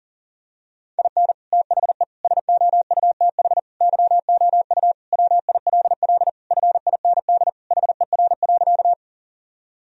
Morse code, heard as a keyed tone, about 37 wpm, 700 Hz. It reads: IN THE SOUTH YOU WILL FIND HER.
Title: Sentences from Top 1000 Words - Review Part 2 - 40wpm